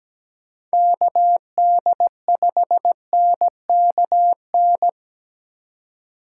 Morse code, heard as a keyed tone, about 17 wpm, 700 Hz. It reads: KD5NKN
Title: USA Call Signs - 6 Characters - 2x Character Spacing - 17wpm